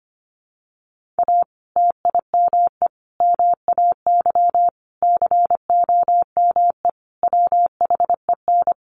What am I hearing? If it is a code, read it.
Morse code, heard as a keyed tone, about 25 wpm, 700 Hz. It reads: A TIME MAY COME WHEN
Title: Sentences from Top 1500 Words 25wpm